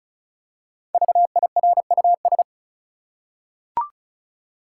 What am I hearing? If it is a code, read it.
Morse code, heard as a keyed tone, about 36 wpm, 700 Hz. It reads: VIRUS
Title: Five Letter Words - 35wpm